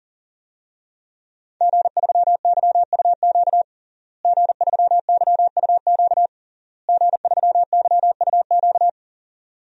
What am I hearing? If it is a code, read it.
Morse code, heard as a keyed tone, about 40 wpm, 700 Hz. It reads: G3YUQ G3YUQ G3YUQ
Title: DXCC-100 Call Signs - 5 Characters - Repeated 3x - 40wpm